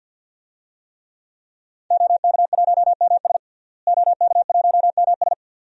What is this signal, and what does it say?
Morse code, heard as a keyed tone, about 50 wpm, 700 Hz. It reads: OK1MS OK1MS
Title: DXCC-100 Call Signs - 5 Characters - Repeated 2x - 50wpm